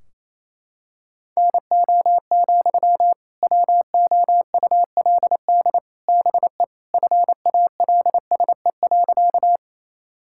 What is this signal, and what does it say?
Morse code, heard as a keyed tone, about 28 wpm, 700 Hz. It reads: NO, WOULD BE FALSE.
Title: Sentences from Top 1600 Words 28wpm